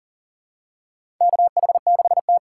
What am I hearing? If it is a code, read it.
Morse code, heard as a keyed tone, about 38 wpm, 700 Hz. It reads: KH6T
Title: USA Call Signs - 4 Characters - 6x Character Spacing - 40wpm